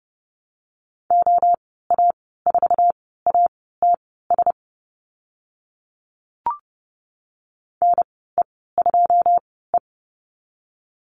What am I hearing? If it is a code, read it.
Morse code, heard as a keyed tone, about 30 wpm, spaced as for 17 wpm, 700 Hz. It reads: OA4ATS NE2E
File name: Lesson 15g - Number 5 - Call Signs 30wpm